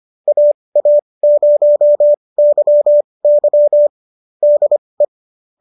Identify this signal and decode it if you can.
Morse code, heard as a keyed tone, about 25 wpm, spaced as for 20 wpm, 570 Hz. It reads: AA0YY DE